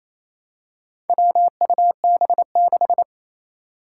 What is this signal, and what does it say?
Morse code, heard as a keyed tone, about 28 wpm, 700 Hz. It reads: WUB6
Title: Four Challenging Letter-Number Combinations 28wpm